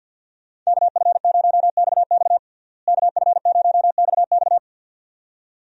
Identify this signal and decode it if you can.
Morse code, heard as a keyed tone, about 52 wpm, 700 Hz. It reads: KF0XX KF0XX
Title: USA Call Signs - 5 Characters - Repeated 2x - 50wpm